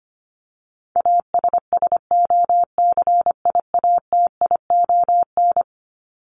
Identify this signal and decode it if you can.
Morse code, heard as a keyed tone, about 25 wpm, 700 Hz. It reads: ASSOCIATION
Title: Top 1001-1250 Words 25wpm